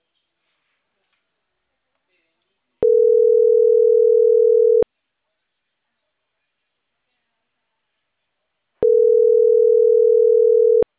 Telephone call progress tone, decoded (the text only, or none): ringback tone